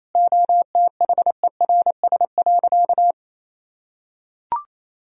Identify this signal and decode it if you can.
Morse code, heard as a keyed tone, about 28 wpm, 700 Hz. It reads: OTHERS.
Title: Sentences from Top 1000 Words - Review Part 4 - 28wpm